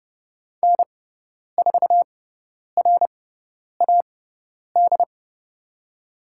Morse code, heard as a keyed tone, about 30 wpm, spaced as for 10 wpm, 700 Hz. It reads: N4RAD
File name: Lesson 18g - Letter m - Call Signs 30wpm